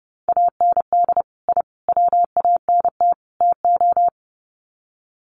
Morse code, heard as a keyed tone, about 30 wpm, 700 Hz. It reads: AND I WANT TO
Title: Sets of 4 Words 30wpm